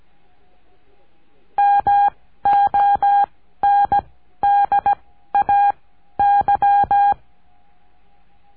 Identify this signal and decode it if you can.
Morse code, heard as a keyed tone, about 17 wpm, spaced as for 12 wpm, 810 Hz. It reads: MONDAY
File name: cursos / CURSO DE TELEGRAFIA / Telegrafia - Trilha 14